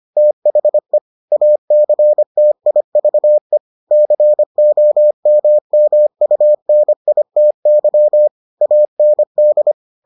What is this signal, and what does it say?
Morse code, heard as a keyed tone, about 25 wpm, 600 Hz. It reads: THE ACTIVE COMMUNITY AND